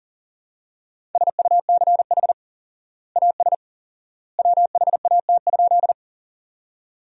Morse code, heard as a keyed tone, about 40 wpm, 700 Hz. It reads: SUCH AS WHAT?
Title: 4x Word-Spacing - Sentences from Top 200 Words - 40wpm